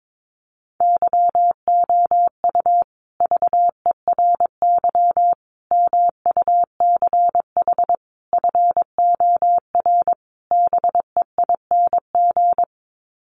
Morse code, heard as a keyed tone, about 22 wpm, 700 Hz. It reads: YOU VERY MUCH FOR BEING
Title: Sets of 5 Words Encore 22wpm